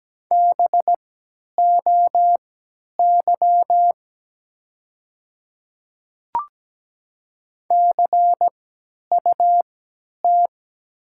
Morse code, heard as a keyed tone, about 17 wpm, spaced as for 10 wpm, 700 Hz. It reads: BOY CUT
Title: 3x Character Spacing - Top 500 Words - 17wpm